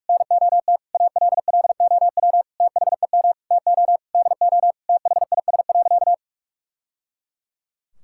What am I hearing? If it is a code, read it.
Morse code, heard as a keyed tone, about 41 wpm, 700 Hz. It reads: NOT ALLOW THEM TO DO THIS.
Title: Sentences from Top 1000 Words - Review Part 5 - 45wpm